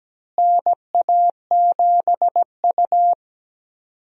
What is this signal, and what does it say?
Morse code, heard as a keyed tone, about 17 wpm, 700 Hz. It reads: NA7U
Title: USA Call Signs - 4 Characters - Repeated 2x - 17wpm